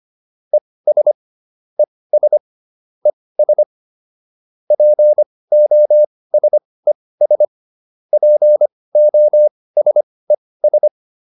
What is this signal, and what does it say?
Morse code, heard as a keyed tone, about 25 wpm, spaced as for 18 wpm, 600 Hz. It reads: ES ES ES POSES POSES